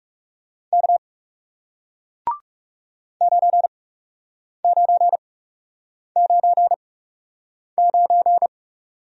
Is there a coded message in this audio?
Yes: Morse code from 0.7 to 8.5 s, heard as a keyed tone, about 38 wpm, spaced as for 8 wpm, 700 Hz. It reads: K 9999